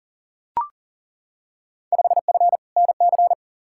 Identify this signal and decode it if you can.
Morse code, heard as a keyed tone, about 40 wpm, 700 Hz. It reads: 5F NC
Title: Field Day Exchanges Encore - 40wpm